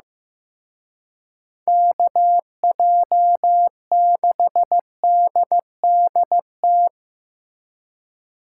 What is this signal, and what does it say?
Morse code, heard as a keyed tone, about 18 wpm, 700 Hz. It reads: KJ6DDT
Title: Speed Racing - USA 2x3 Call Signs 15wpm